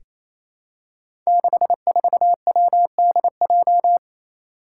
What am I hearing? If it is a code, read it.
Morse code, heard as a keyed tone, about 29 wpm, 700 Hz. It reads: 64WDJ
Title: Five Challenging Letter-Number Combinations 28wpm